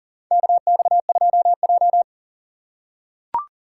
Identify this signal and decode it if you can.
Morse code, heard as a keyed tone, about 38 wpm, 700 Hz. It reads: KX2J
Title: USA Call Signs - 4 Characters - 6x Character Spacing - 40wpm